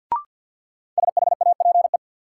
Morse code, heard as a keyed tone, about 53 wpm, 700 Hz. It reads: SHAPE
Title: Sets of 5 Words Encore - New Word Drill - 50wpm